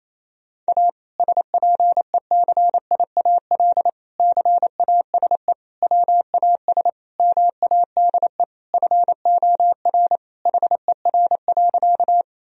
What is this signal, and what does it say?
Morse code, heard as a keyed tone, about 28 wpm, 700 Hz. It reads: A SPECIAL CASE WAS MADE FOR HER.
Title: Sentences from Top 1000 Words - Review Part 3 - 28wpm